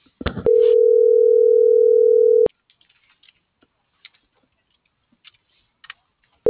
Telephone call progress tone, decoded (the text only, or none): ringback tone